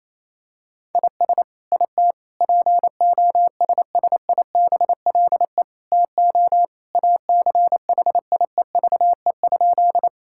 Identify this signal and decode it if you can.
Morse code, heard as a keyed tone, about 28 wpm, 700 Hz. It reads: IS IT POSSIBLE TO ACHIEVE?
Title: Sentences from Top 1000 Words - Review Part 6 - 28wpm